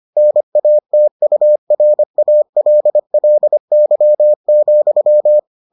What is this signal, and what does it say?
Morse code, heard as a keyed tone, about 25 wpm, 600 Hz. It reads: NATURALLY,